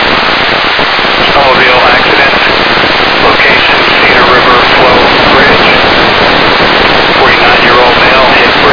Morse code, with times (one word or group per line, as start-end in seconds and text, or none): none